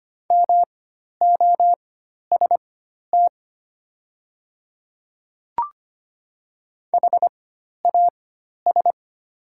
Morse code, heard as a keyed tone, about 25 wpm, spaced as for 12 wpm, 700 Hz. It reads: MOST HAS
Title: 4x Character Spacing - Top 100 Words - 25wpm